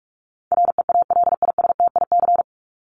Morse code, heard as a keyed tone, about 44 wpm, 700 Hz. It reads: REALISTIC
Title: Top 1751-2000 Words 45wpm